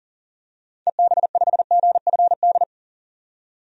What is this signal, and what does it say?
Morse code, heard as a keyed tone, about 40 wpm, 700 Hz. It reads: EB5GFD
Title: DXCC-100 Call Signs - 6 Characters - 40wpm